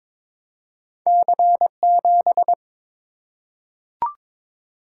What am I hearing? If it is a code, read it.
Morse code, heard as a keyed tone, about 22 wpm, 700 Hz. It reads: C7
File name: practice / 22wpm / Two Letter-Number Combinations - 22wpm